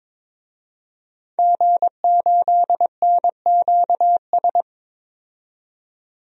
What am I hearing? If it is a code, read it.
Morse code, heard as a keyed tone, about 22 wpm, 700 Hz. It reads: G8NQS
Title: DXCC-100 Call Signs - 5 Characters - 22wpm